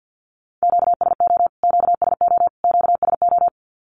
Morse code, heard as a keyed tone, about 50 wpm, 700 Hz. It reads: QSO QSO QSO
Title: Q-Codes Repeated 3x 50wpm